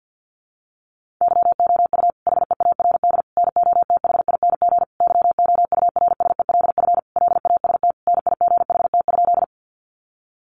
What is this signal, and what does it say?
Morse code, heard as a keyed tone, about 50 wpm, 700 Hz. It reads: YOU HEARD NOTHING YOURSELF LAST NIGHT?
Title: Sentences from Top 600 Words 50wpm